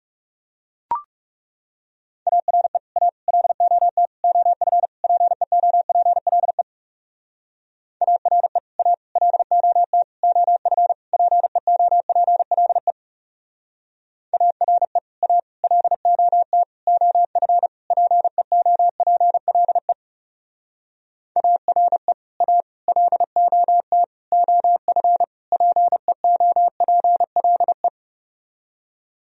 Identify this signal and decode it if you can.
Morse code, heard as a keyed tone, about 37 wpm, 700 Hz. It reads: ARE A LOT OF PEOPLE ARE A LOT OF PEOPLE ARE A LOT OF PEOPLE ARE A LOT OF PEOPLE
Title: Speed-Racing - Sets of 5 Words 30wpm